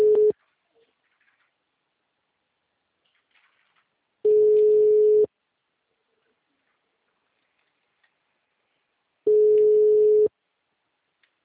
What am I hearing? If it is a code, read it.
Ringback tone (Europe, 425 Hz).
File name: out- -2036- - - .16053